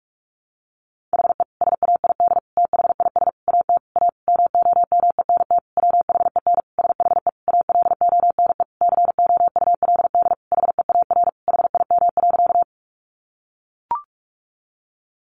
Morse code, heard as a keyed tone, about 44 wpm, 700 Hz. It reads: HE SAID THIS AT A MOMENT WHEN SHE ALONE COULD HEAR HIM.
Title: Sentences from Top 1000 Words 45wpm